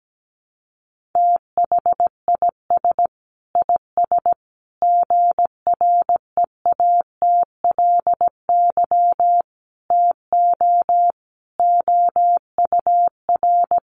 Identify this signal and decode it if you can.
Morse code, heard as a keyed tone, about 17 wpm, 700 Hz. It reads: THIS IS GREATLY TO OUR
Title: Sentences from Top 1400 Words 17wpm